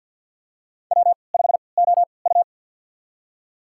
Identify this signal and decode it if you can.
Morse code, heard as a keyed tone, about 51 wpm, spaced as for 28 wpm, 700 Hz. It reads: W5OU
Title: USA Call Signs - 4 Characters - 3x Character Spacing - 50wpm